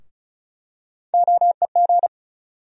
Morse code, heard as a keyed tone, about 34 wpm, 700 Hz. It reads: OEG
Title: USA Call Sign Postfixes - 3 Letters 35wpm